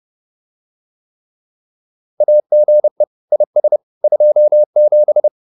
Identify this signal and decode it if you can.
Morse code, heard as a keyed tone, about 30 wpm, 600 Hz. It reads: AGE IS 27
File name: QSO202 30